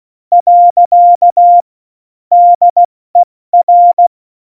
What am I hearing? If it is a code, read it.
Morse code, heard as a keyed tone, about 16 wpm, 700 Hz. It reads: . DER
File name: Morsekurs / Witze / Witz 18. Juni 2019, 70 F80